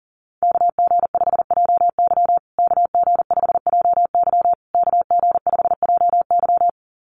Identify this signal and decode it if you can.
Morse code, heard as a keyed tone, about 40 wpm, 700 Hz. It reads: KG5JY KG5JY KG5JY
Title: USA Call Signs Repeated 3x 40wpm